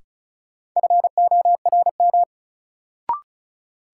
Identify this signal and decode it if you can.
Morse code, heard as a keyed tone, about 36 wpm, 700 Hz. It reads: FORM
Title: Sets of 4 Words - New Word Drill - 35wpm